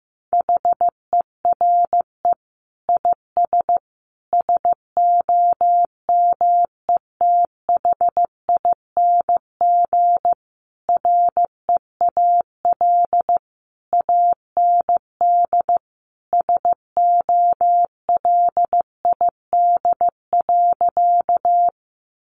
Morse code, heard as a keyed tone, about 15 wpm, 700 Hz. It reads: HERE IS SOMETHING REAL AND SOLID.
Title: Sentences from Top 1500 Words 15wpm